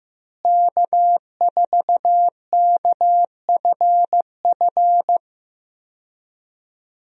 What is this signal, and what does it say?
Morse code, heard as a keyed tone, about 15 wpm, 700 Hz. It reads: K4KFF
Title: USA Call Signs - 5 Characters - Repeated 3x - 15wpm